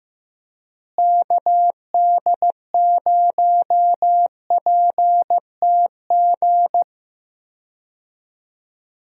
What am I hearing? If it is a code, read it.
Morse code, heard as a keyed tone, about 15 wpm, 700 Hz. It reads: KD0PTG
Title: USA Call Signs - 6 Characters - Repeated 3x - 15wpm